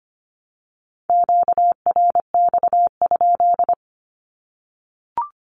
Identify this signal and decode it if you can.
Morse code, heard as a keyed tone, about 25 wpm, 700 Hz. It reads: QRX?
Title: Q-Codes 25wpm